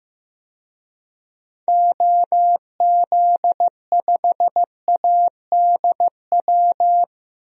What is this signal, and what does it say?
Morse code, heard as a keyed tone, about 15 wpm, 700 Hz. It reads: OZ5ADW
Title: DXCC-100 Call Signs - 6 Characters - Repeated 3x - 15wpm